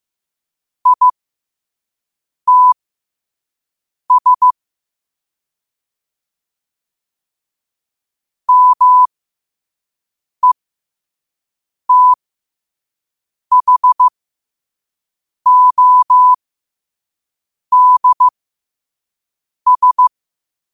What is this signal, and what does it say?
Morse code, heard as a keyed tone, about 15 wpm, spaced as for 5 wpm, 1000 Hz. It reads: ITS METHODS
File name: qotdfiles / 5wpm